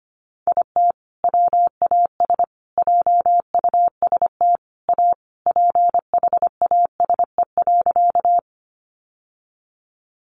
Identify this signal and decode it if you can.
Morse code, heard as a keyed tone, about 25 wpm, 700 Hz. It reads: IT WAS JUST A PHASE.
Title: Sentences from Top 1200 Words 25wpm